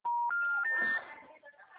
Special information tone, with noise.